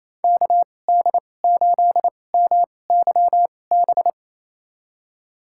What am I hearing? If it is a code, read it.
Morse code, heard as a keyed tone, about 28 wpm, spaced as for 20 wpm, 700 Hz. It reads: KD8MYB
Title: USA Call Signs - 2x Character Spacing - 28wpm